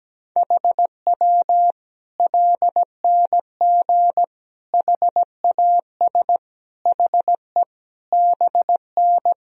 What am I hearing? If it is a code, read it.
Morse code, heard as a keyed tone, about 17 wpm, 700 Hz. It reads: HW LNG HAS HE BN
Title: Sentences from Top 100 Words and Abbreviations 17wpm